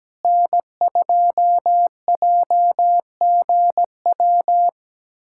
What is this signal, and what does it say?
Morse code, heard as a keyed tone, about 17 wpm, 700 Hz. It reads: N2JGW
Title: USA Callsign DE Callsign - 5 Characters - 17wpm